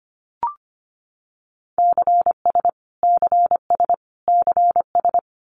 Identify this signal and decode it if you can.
Morse code, heard as a keyed tone, about 25 wpm, 700 Hz. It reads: CS CS CS
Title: Abbreviations Repeated 3x - 25wpm